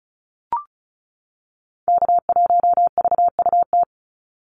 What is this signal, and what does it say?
Morse code, heard as a keyed tone, about 34 wpm, 700 Hz. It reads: K1VUT
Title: Call Signs - 35wpm